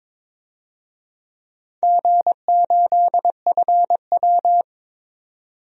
Morse code, heard as a keyed tone, about 22 wpm, 700 Hz. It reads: G8FW
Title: DXCC-100 Call Signs - 4 Characters - 22wpm